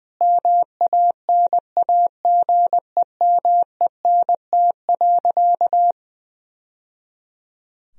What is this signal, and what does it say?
Morse code, heard as a keyed tone, about 19 wpm, 700 Hz. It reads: MANAGEMENT.
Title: Sentences from Top 1000 Words - Review Part 4 - 20wpm